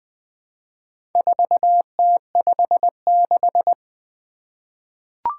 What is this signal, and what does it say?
Morse code, heard as a keyed tone, about 20 wpm, 700 Hz. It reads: 4T56
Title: Four Challenging Letter-Number Combinations 20wpm